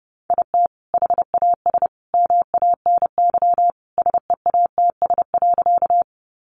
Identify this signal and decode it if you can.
Morse code, heard as a keyed tone, about 30 wpm, 700 Hz. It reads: IT HAS MANY SEATS.
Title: Additional Easy Phrases 30wpm